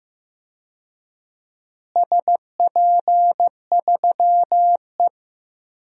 Morse code, heard as a keyed tone, about 16 wpm, 700 Hz. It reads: SP3E
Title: DXCC-100 Call Signs - 4 Characters - Repeated 3x - 15wpm